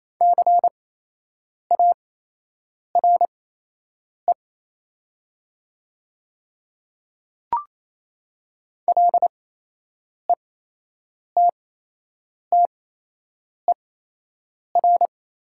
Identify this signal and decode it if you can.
Morse code, heard as a keyed tone, about 28 wpm, spaced as for 8 wpm, 700 Hz. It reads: CARE LETTER
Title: 8x Character Spacing - Top 300 Words - 28wpm